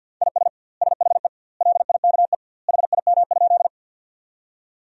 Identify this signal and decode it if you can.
Morse code, heard as a keyed tone, about 50 wpm, 700 Hz. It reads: IS SHE LIKE HIM?
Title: 2x Word-Spacing - Sentences from Top 100 Words - 50wpm